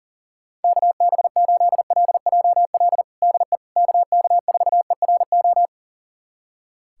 Morse code, heard as a keyed tone, about 40 wpm, 700 Hz. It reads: KB8LJL DE KK4ERO
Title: USA Callsign DE Callsign 40wpm